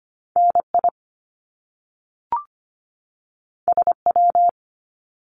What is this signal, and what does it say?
Morse code, heard as a keyed tone, about 25 wpm, 700 Hz. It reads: NI SW
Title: Top Two Letter Combinations in Words 25wpm